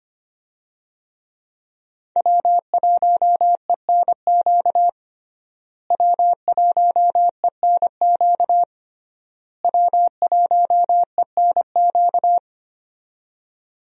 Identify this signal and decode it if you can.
Morse code, heard as a keyed tone, about 25 wpm, 700 Hz. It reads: W1ENQ W1ENQ W1ENQ
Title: USA Call Signs - 5 Characters - Repeated 3x - 25wpm